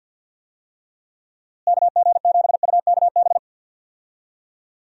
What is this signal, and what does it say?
Morse code, heard as a keyed tone, about 52 wpm, 700 Hz. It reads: KG7UKB
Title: USA Call Signs - 6 Characters - Repeated 3x - 50wpm